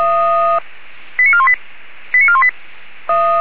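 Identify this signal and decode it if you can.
Five-tone selective call (ZVEI), heard as quick runs of five tones, tones of about 70 ms. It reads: 89418 89418
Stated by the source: ZVEI five-tone selective call received on SDR.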